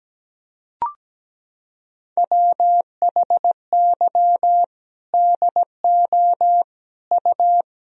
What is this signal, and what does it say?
Morse code, heard as a keyed tone, about 17 wpm, 700 Hz. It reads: WHY DO U
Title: Sentences from Top 300 Words and Abbreviations 17wpm